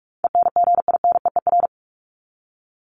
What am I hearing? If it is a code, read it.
Morse code, heard as a keyed tone, about 44 wpm, 700 Hz. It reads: ENGINEER
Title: ING Suffix - Top 100 Words with Base Words 45wpm